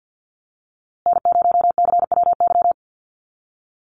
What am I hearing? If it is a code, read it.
Morse code, heard as a keyed tone, about 50 wpm, 700 Hz. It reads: N0CWY
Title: USA Call Signs Repeated 3x 50wpm